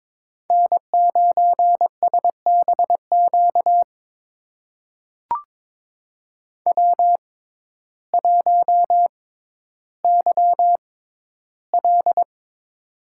Morse code, heard as a keyed tone, about 23 wpm, spaced as for 12 wpm, 700 Hz. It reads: N9SBQ W1YL
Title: USA Call Signs - 6x Character Spacing - 22wpm